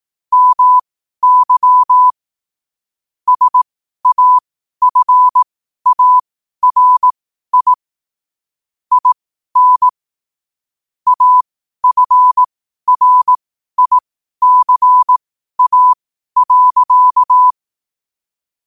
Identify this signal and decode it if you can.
Morse code, heard as a keyed tone, about 18 wpm, spaced as for 12 wpm, 1000 Hz. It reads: MY SAFARI IN AFRICA.
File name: qotdfiles / 12wpm